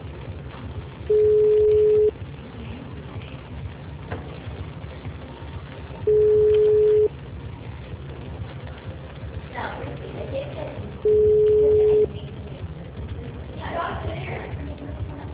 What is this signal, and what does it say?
Ringback tone (Europe, 425 Hz), with speech.
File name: out- -2008- - - .4698